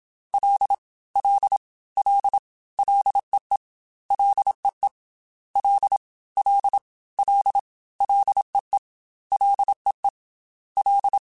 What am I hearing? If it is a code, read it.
Morse code, heard as a keyed tone, about 27 wpm, 800 Hz. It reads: L L L LEE LEE L L L LEE LEE L
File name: Ls / Single L m3s / Single L Prefixe 3x2 Twice @25